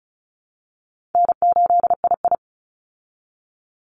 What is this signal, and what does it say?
Morse code, heard as a keyed tone, about 34 wpm, 700 Hz. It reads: N8II
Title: Call Signs Repeated 3x - 35wpm